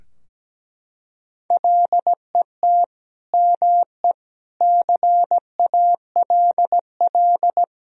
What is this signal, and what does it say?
Morse code, heard as a keyed tone, about 17 wpm, 700 Hz. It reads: LET ME CALL
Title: Sentences from Top 1000 Words - Review Part 2 - 17wpm